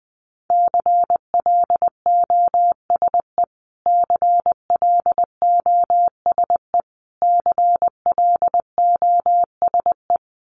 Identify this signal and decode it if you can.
Morse code, heard as a keyed tone, about 20 wpm, 700 Hz. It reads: CLOSE CLOSE CLOSE
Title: Top 500 Words Repeated 20wpm